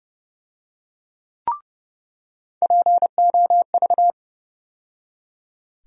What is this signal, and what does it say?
Morse code, heard as a keyed tone, about 29 wpm, 700 Hz. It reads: POV
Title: USA Call Sign Postfixes - 3 Letters 30wpm